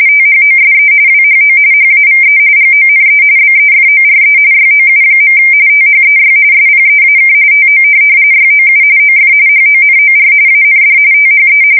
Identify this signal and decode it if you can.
RTTY, about 45.45 baud. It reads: THOSE WHO HAVE GONE BEFORE YOU. THEY SPEAK TO YOU OF HONOR AND THE
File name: fsk170 / USS-Missouri